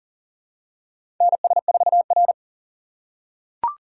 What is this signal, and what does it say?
Morse code, heard as a keyed tone, about 40 wpm, 700 Hz. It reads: NS4R
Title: USA Call Signs - 4x Character Spacing - 40wpm